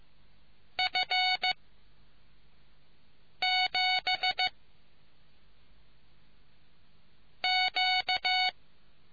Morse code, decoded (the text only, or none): F 7 Q